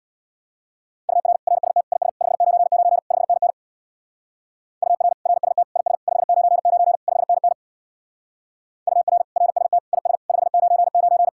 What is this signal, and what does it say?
Morse code, heard as a keyed tone, about 59 wpm, 700 Hz. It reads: UR RST IS 599 5NN UR RST IS 599 5NN UR RST IS 599
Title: Speed-Racing Common QSO Phrases - 50wpm